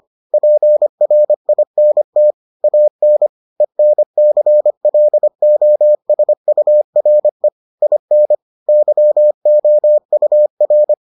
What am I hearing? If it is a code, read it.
Morse code, heard as a keyed tone, about 25 wpm, 600 Hz. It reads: PRINT AN ENCLOSURE IN YOUR